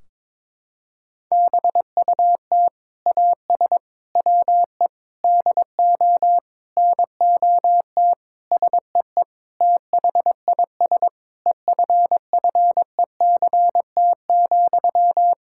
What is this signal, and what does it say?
Morse code, heard as a keyed tone, about 22 wpm, 700 Hz. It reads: BUT AS WE DO NOT SEE THIS EFFECT,
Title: Sentences from Top 1600 Words 22wpm